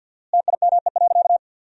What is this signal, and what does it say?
Morse code, heard as a keyed tone, about 50 wpm, 700 Hz. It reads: TIME.
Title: 4x Word-Spacing - Sentences from Top 200 Words - 50wpm